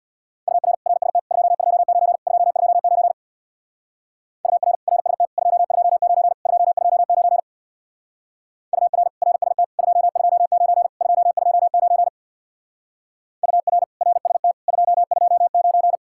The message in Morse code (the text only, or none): UR RST 229 229 UR RST 229 229 UR RST 229 229 UR RST 229